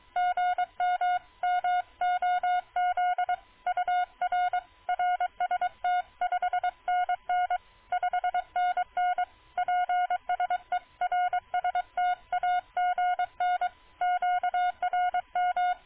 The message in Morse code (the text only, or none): GM MOZ UR RST 5NN 5NN PSE RST AGN QRM